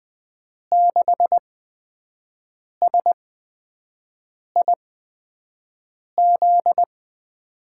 Morse code, heard as a keyed tone, about 20 wpm, spaced as for 5 wpm, 700 Hz. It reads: 6SIZ
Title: USA Call Signs - 6 Characters - 8x Character Spacing - 20wpm